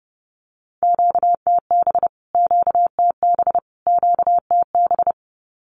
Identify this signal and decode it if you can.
Morse code, heard as a keyed tone, about 30 wpm, 700 Hz. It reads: QTB QTB QTB
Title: Q-Codes Repeated 3x 30wpm